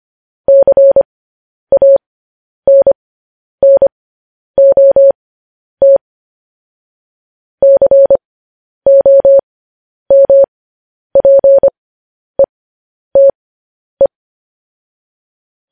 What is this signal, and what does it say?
Morse code, heard as a keyed tone, about 25 wpm, spaced as for 10 wpm, 550 Hz. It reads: CANNOT COMPETE
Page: news-headlines-in-morse-code-at-10-wpm